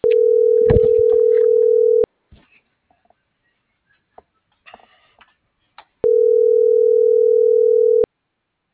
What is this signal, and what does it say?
Ringback tone (North America), 440 + 480 Hz.